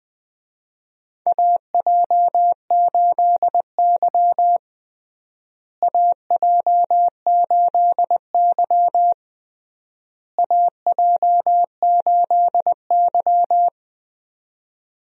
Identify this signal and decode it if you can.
Morse code, heard as a keyed tone, about 21 wpm, 700 Hz. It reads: AJ8Y AJ8Y AJ8Y
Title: USA Call Signs - 4 Characters - Repeated 3x - 20wpm